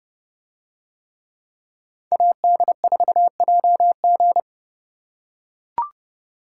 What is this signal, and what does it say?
Morse code, heard as a keyed tone, about 29 wpm, 700 Hz. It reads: AD4JG